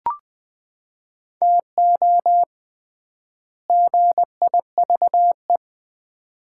Morse code, heard as a keyed tone, about 21 wpm, 700 Hz. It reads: TO GIVE
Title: 3x Word-Spacing - Sets of 2 Words - 20wpm